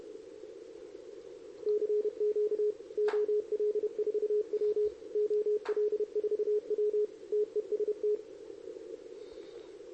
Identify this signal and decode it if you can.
Morse code, heard as a keyed tone, about 30 wpm, 420 Hz. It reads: CQ OL4W OL4W TEST
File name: CQWWCW2023 / RUN